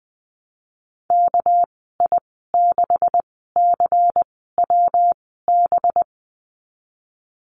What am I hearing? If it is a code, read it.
Morse code, heard as a keyed tone, about 20 wpm, 700 Hz. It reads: K I 6 C W B
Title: 200 Lesson 32g - Letter k - Call Signs 20wpm